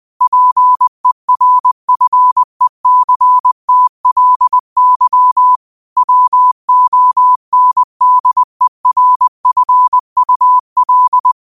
Morse code, heard as a keyed tone, about 20 wpm, 1000 Hz. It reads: PERFECTLY WONDERFUL
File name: qotdfiles / 20wpm